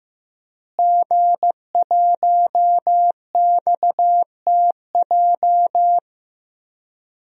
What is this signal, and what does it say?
Morse code, heard as a keyed tone, about 15 wpm, 700 Hz. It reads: G1XTJ
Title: DXCC-100 Call Signs - 5 Characters - Repeated 3x - 15wpm